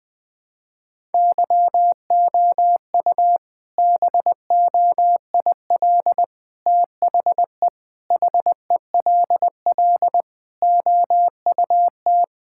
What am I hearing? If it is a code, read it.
Morse code, heard as a keyed tone, about 20 wpm, 700 Hz. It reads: YOU BOIL THE HELL OUT
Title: All-Age Appropriate Jokes - 20wpm